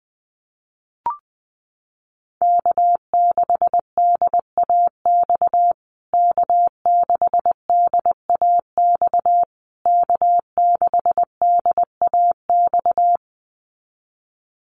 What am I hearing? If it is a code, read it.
Morse code, heard as a keyed tone, about 20 wpm, 700 Hz. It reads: K6DAX K6DAX K6DAX
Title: USA Call Signs Repeated 3x 20wpm